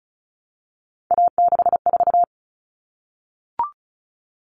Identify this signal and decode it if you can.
Morse code, heard as a keyed tone, about 34 wpm, 700 Hz. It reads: A64
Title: All Cut Numbers - 3-Digit 35wpm